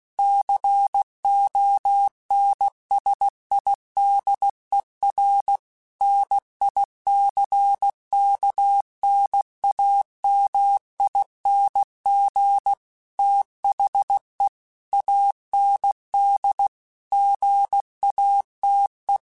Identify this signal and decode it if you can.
Morse code, heard as a keyed tone, about 16 wpm, 800 Hz. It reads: CONSIDER NICKNAMING THE AND GATE
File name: Technical Subjects / Technical Subjects-mp3s / AND Gate @15